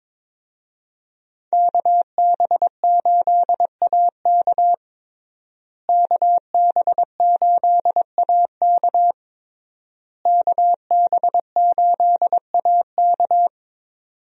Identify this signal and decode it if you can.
Morse code, heard as a keyed tone, about 22 wpm, 700 Hz. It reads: KB8AK KB8AK KB8AK
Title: USA Call Signs - 5 Characters - Repeated 3x - 22wpm